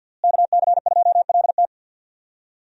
Morse code, heard as a keyed tone, about 52 wpm, 700 Hz. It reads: KC2LT
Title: USA Call Signs - 5 Characters - Repeated 2x - 50wpm